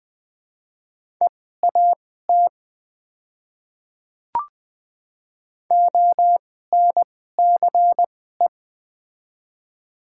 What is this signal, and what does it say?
Morse code, heard as a keyed tone, about 20 wpm, spaced as for 14 wpm, 700 Hz. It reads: EAT ONCE
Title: 2x Character Spacing - Top 300 Words - 20wpm